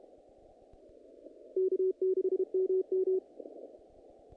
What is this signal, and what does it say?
Morse code, heard as a keyed tone, about 31 wpm, 380 Hz. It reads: K6MM
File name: contests / naqp-cw-k5tr-op-k5pi- -2000